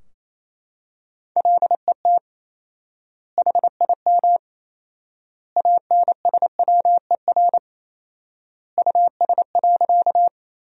Morse code, heard as a keyed tone, about 28 wpm, 700 Hz. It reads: LET HIM ANSWER US.
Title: 4x Word-Spacing - Sentences from Top 300 Words - 28wpm